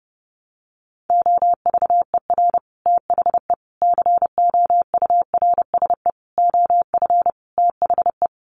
Morse code, heard as a keyed tone, about 30 wpm, 700 Hz. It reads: OVER THE COURSE OF THE
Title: Sets of 5 Words 30wpm